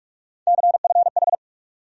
Morse code, heard as a keyed tone, about 45 wpm, 700 Hz. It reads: CFH
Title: USA Call Sign Postfixes - 3 Letters 45wpm